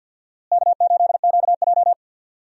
Morse code, heard as a keyed tone, about 46 wpm, 700 Hz. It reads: K8QJ
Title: USA Call Signs - 4 Characters - Repeated 3x - 50wpm